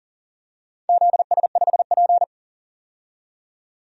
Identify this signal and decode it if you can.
Morse code, heard as a keyed tone, about 37 wpm, 700 Hz. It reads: ZS5P
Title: DXCC-100 Call Signs - 4 Characters - 40wpm